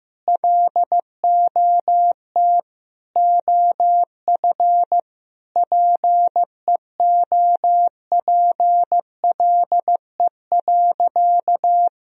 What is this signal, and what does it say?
Morse code, heard as a keyed tone, about 16 wpm, 700 Hz. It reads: LOT OF PEOPLE.
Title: Sentences from Top 1000 Words - Review Part 3 - 15wpm